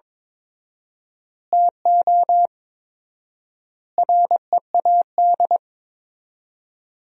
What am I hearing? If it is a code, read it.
Morse code, heard as a keyed tone, about 22 wpm, 700 Hz. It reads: TO READ
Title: 4x Word-Spacing - Sentences from Top 500 Words - 22wpm